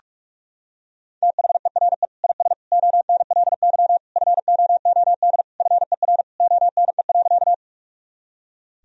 Morse code, heard as a keyed tone, about 41 wpm, 700 Hz. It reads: THERE IS ONLY FOOD FER ONE.
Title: Sentences from Top 300 Words and Abbreviations 45wpm